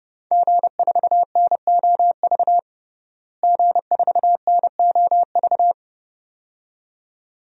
Morse code, heard as a keyed tone, about 32 wpm, 700 Hz. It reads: G4NOV G4NOV
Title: DXCC-100 Call Signs - 5 Characters - Repeated 2x - 30wpm